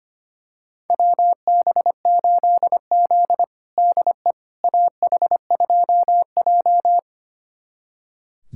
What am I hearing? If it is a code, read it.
Morse code, heard as a keyed tone, about 25 wpm, 700 Hz. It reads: WB8Z DE AH2J
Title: USA Callsign DE Callsign - 4 Characters - 25wpm